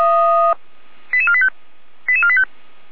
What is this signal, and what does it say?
Five-tone selective call (ZVEI), heard as quick runs of five tones, tones of about 80 ms. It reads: 88485 88485